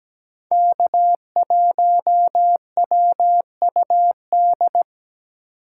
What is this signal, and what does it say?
Morse code, heard as a keyed tone, about 17 wpm, 700 Hz. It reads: K1WUD
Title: USA Call Signs - 5 Characters - 4x Character Spacing - 17wpm